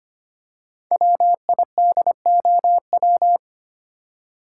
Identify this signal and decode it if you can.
Morse code, heard as a keyed tone, about 25 wpm, 700 Hz. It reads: WIDOW
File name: 5 Letter Words - First 3 Letters Given - 25wpm